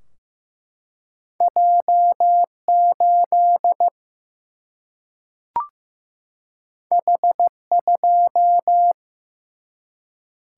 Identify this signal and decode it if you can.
Morse code, heard as a keyed tone, about 15 wpm, 700 Hz. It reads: J8 H2
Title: Single-Letter Single-Number 15wpm